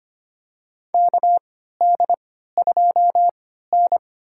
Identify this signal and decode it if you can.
Morse code, heard as a keyed tone, about 25 wpm, spaced as for 14 wpm, 700 Hz. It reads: KD2N
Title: USA Call Signs - 4 Characters - 3x Character Spacing - 25wpm